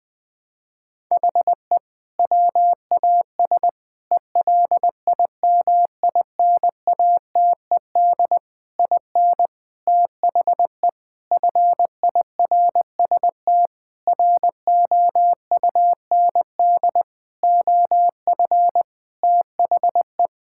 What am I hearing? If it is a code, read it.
Morse code, heard as a keyed tone, about 20 wpm, 700 Hz. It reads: HE WAS ELIMINATED IN THE FIRST ROUND OF THE
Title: Sentences with 6-Word Phrases - Spoken Beginning - 20wpm